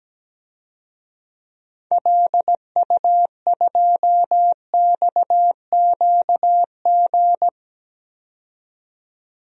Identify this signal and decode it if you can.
Morse code, heard as a keyed tone, about 17 wpm, 700 Hz. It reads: LU2XQG
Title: DXCC-100 Call Signs - 6 Characters - 17wpm